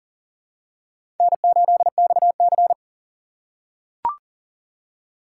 Morse code, heard as a keyed tone, about 40 wpm, 700 Hz. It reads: N8XC
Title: USA Call Signs - 4x Character Spacing - 40wpm